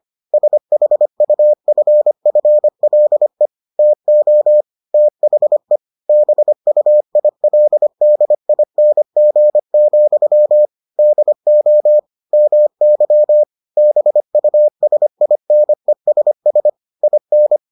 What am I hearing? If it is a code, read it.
Morse code, heard as a keyed tone, about 25 wpm, 600 Hz. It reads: SHUFFLE TO THE BUILDING, DO MY BUSINESS IN